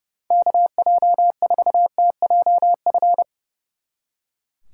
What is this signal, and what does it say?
Morse code, heard as a keyed tone, about 36 wpm, 700 Hz. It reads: KJ4TJF
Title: Speed Racing - USA Call Signs 30wpm